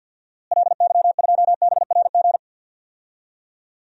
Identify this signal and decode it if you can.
Morse code, heard as a keyed tone, about 49 wpm, 700 Hz. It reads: LY2BRG
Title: DXCC-100 Call Signs - 6 Characters - Repeated 3x - 50wpm